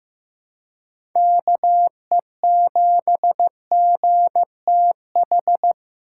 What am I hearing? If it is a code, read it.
Morse code, heard as a keyed tone, about 15 wpm, 700 Hz. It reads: KE7GTH
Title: USA Call Signs - 6 Characters - 4x Character Spacing - 15wpm